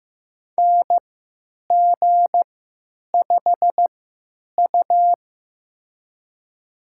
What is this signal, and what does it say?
Morse code, heard as a keyed tone, about 16 wpm, spaced as for 9 wpm, 700 Hz. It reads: NG5U
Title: USA Call Signs - 4 Characters - 3x Character Spacing - 15wpm